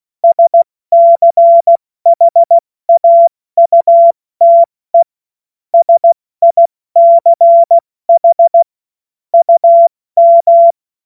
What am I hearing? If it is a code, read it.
Morse code, heard as a keyed tone, about 16 wpm, 680 Hz. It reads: SCHAUTE SICH UM
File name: Morsekurs / Witze / Witz 08. November 2022 70 F80